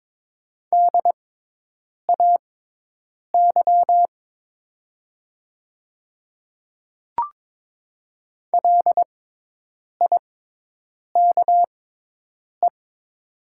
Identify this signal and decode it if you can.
Morse code, heard as a keyed tone, about 22 wpm, spaced as for 8 wpm, 700 Hz. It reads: DAY LIKE
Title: 6x Character Spacing - Top 300 Words - 22wpm